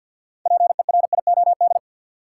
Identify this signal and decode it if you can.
Morse code, heard as a keyed tone, about 52 wpm, 700 Hz. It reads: PERIOD
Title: Six Letter Words - 50wpm